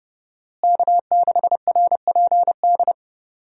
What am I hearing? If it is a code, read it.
Morse code, heard as a keyed tone, about 30 wpm, 700 Hz. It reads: K6RPD
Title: USA Call Signs - 5 Characters - Repeated 2x - 30wpm